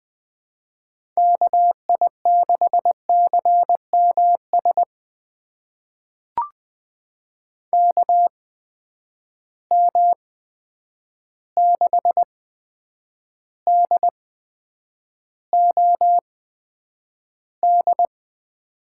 Morse code, heard as a keyed tone, about 20 wpm, spaced as for 9 wpm, 700 Hz. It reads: KI6CMS KM6DOD
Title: USA Call Signs - 6 Characters - 8x Character Spacing - 20wpm